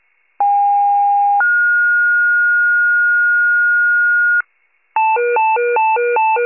Two-tone page, heard as a one-second tone, then a three-second tone, about 810 then 1430 Hz, with noise. It is heard from 0.4 to 4.4 s.